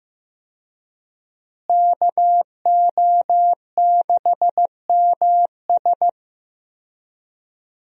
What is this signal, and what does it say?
Morse code, heard as a keyed tone, about 15 wpm, 700 Hz. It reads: KO6MS
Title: USA Call Signs - 5 Characters - Repeated 2x - 15wpm